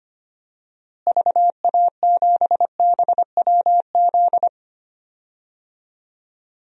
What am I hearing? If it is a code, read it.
Morse code, heard as a keyed tone, about 25 wpm, 700 Hz. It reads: VA7BWZ
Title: DXCC-100 Call Signs - 6 Characters - Repeated 2x - 25wpm